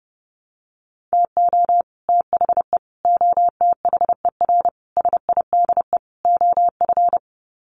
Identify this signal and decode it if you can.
Morse code, heard as a keyed tone, about 30 wpm, 700 Hz. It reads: TO THE OTHER SIDE OF
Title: Sets of 5 Words Encore 30wpm